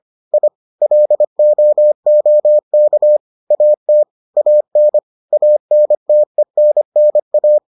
Morse code, heard as a keyed tone, about 25 wpm, 600 Hz. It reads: I LOOK AT AN ANTENNA